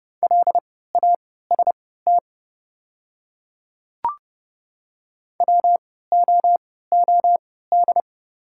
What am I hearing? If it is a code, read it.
Morse code, heard as a keyed tone, about 30 wpm, spaced as for 17 wpm, 700 Hz. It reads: LAST WOOD
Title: 3x Character Spacing - Top 300 Words - 30wpm